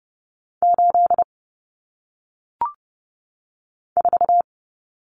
Morse code, heard as a keyed tone, about 30 wpm, 700 Hz. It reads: 8 4
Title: Numbers - 1-Digit 30wpm